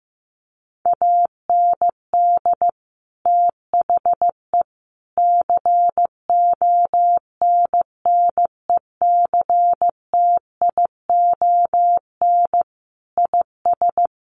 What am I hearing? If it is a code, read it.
Morse code, heard as a keyed tone, about 15 wpm, 700 Hz. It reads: AND THE CONNECTION IS
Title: Sentences from Top 1300 Words 15wpm